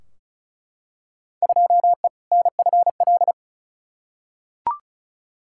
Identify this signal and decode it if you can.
Morse code, heard as a keyed tone, about 34 wpm, 700 Hz. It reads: 2E NFL
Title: Field Day Exchanges Encore - 35wpm